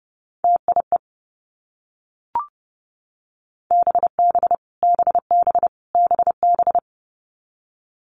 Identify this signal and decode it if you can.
Morse code, heard as a keyed tone, about 30 wpm, 700 Hz. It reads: TIE BB BB BB
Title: Multi-letter Phonemes Repeated 3x 30wpm